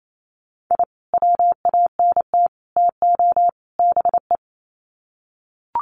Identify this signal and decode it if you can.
Morse code, heard as a keyed tone, about 28 wpm, 700 Hz. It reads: I WANT TO BE
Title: Sets of 4 Words 28wpm